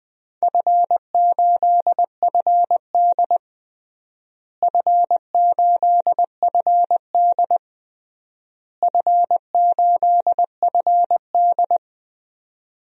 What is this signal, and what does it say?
Morse code, heard as a keyed tone, about 21 wpm, 700 Hz. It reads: F8FD F8FD F8FD
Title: DXCC-100 Call Signs - 4 Characters - Repeated 3x - 20wpm